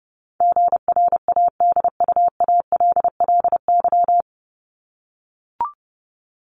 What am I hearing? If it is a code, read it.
Morse code, heard as a keyed tone, about 30 wpm, 700 Hz. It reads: GRADUALLY
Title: LY Suffix - Top 100 Words with Base Words 30wpm